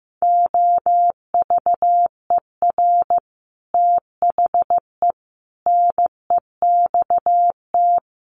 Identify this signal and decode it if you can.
Morse code, heard as a keyed tone, about 15 wpm, 700 Hz. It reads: OVER THE NEXT
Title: Sets of 5 Words Encore 15wpm